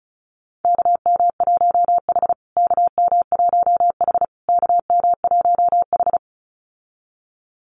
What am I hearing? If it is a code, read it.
Morse code, heard as a keyed tone, about 34 wpm, 700 Hz. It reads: KM1H KM1H KM1H
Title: USA Call Signs Repeated 3x 35wpm